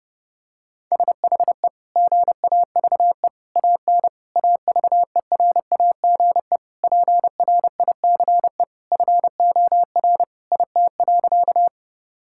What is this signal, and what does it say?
Morse code, heard as a keyed tone, about 29 wpm, 700 Hz. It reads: SHE GAVE AN AVERAGE PRICE FOR IT.
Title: Sentences from Top 1000 Words - Review Part 6 - 30wpm